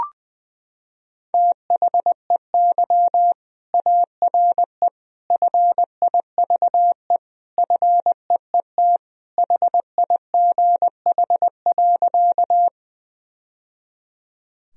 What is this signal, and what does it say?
Morse code, heard as a keyed tone, about 20 wpm, 700 Hz. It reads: THEY ARE FIVE FEET HIGH.
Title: Sentences from Top 1000 Words - Review Part 2 - 20wpm